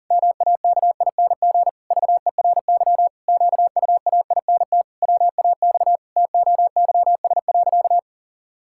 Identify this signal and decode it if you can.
Morse code, heard as a keyed tone, about 40 wpm, 700 Hz. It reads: MAKING VERY QUAINT WAX TOYS.
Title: Pangram Challenge v2 - 40wpm